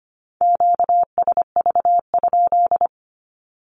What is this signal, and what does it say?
Morse code, heard as a keyed tone, about 25 wpm, 700 Hz. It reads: QSV?
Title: Q-Codes with Spoken Meaning - 25wpm